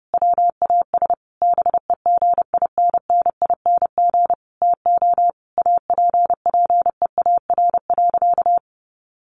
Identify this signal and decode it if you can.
Morse code, heard as a keyed tone, about 28 wpm, 700 Hz. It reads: WAS BEGINNING TO APPEAR.
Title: Sentences from Top 1200 Words 30wpm